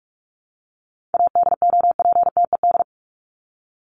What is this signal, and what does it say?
Morse code, heard as a keyed tone, about 44 wpm, 700 Hz. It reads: ADOPTED
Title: ED Suffix - Top 100 Words with Base Words 45wpm